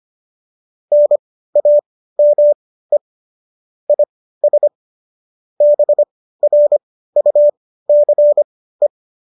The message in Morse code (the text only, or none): NAME IS BRUCE